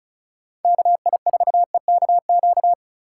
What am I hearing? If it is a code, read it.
Morse code, heard as a keyed tone, about 34 wpm, 700 Hz. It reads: KI4EKQ
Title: USA Call Signs - 6 Characters - 4x Character Spacing - 35wpm